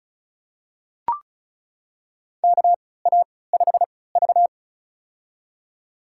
Morse code, heard as a keyed tone, about 36 wpm, spaced as for 20 wpm, 700 Hz. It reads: KA5V
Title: USA Call Signs - 4 Characters - 3x Character Spacing - 35wpm